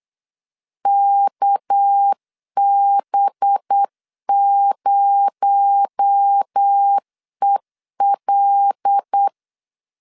Morse code, heard as a keyed tone, about 8 wpm, 800 Hz. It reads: KB0EL
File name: LEZIONI / 6 PKX0 / 6 PKX0 8wpm 3341